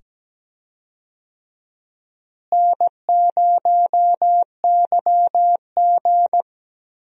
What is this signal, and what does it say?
Morse code, heard as a keyed tone, about 17 wpm, 700 Hz. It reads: N0YG
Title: USA Call Signs - 4 Characters - Repeated 2x - 17wpm